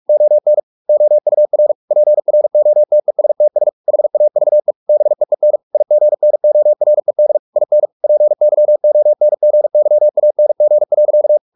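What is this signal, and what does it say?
Morse code, heard as a keyed tone, about 44 wpm, 600 Hz. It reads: ON OUR PROTESTS HAVE BEEN IGNORED IN PYONGYANG.
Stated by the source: Morse practice at 45 wpm, the novel Never by Ken Follett in English, from CW med Gnister.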